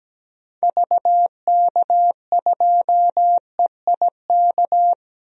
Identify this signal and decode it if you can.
Morse code, heard as a keyed tone, about 17 wpm, 700 Hz. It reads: VK2EIK
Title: DXCC-100 Call Signs - 6 Characters - 17wpm